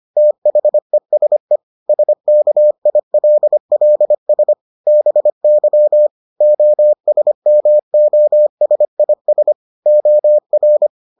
Morse code, heard as a keyed tone, about 25 wpm, 600 Hz. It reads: THESE SKILLS BY OSMOSIS OR